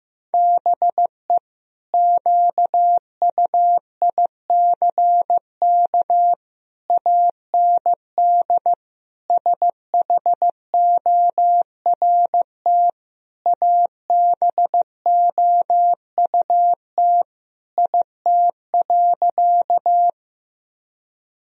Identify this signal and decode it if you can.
Morse code, heard as a keyed tone, about 16 wpm, 700 Hz. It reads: BE QUICK AND SHORT ABOUT IT.
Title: Sentences from Top 1000 Words - Review Part 2 - 15wpm